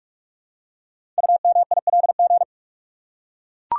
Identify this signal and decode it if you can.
Morse code, heard as a keyed tone, about 44 wpm, 700 Hz. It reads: UMILG
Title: Code Groups - 5 Letters - 45wpm